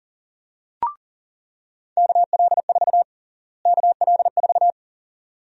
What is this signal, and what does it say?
Morse code, heard as a keyed tone, about 39 wpm, 700 Hz. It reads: KL4 KL4
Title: USA Call Sign Prefixes - Repeated 2x - 40wpm